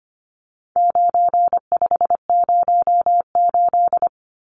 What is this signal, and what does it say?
Morse code, heard as a keyed tone, about 25 wpm, 700 Hz. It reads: 9508
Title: Numbers - 4-Digit 25wpm